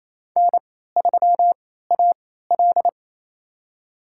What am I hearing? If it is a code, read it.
Morse code, heard as a keyed tone, about 28 wpm, spaced as for 16 wpm, 700 Hz. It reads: N3AL
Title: USA Call Signs - 4 Characters - 3x Character Spacing - 28wpm